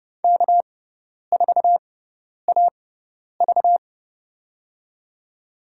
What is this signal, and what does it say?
Morse code, heard as a keyed tone, about 30 wpm, spaced as for 10 wpm, 700 Hz. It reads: K4AV
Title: USA Call Signs - 6x Character Spacing - 30wpm